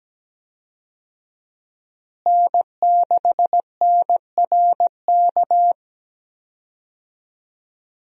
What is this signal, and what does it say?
Morse code, heard as a keyed tone, about 17 wpm, 700 Hz. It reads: N6NRK
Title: USA Call Signs - 5 Characters - Repeated 2x - 17wpm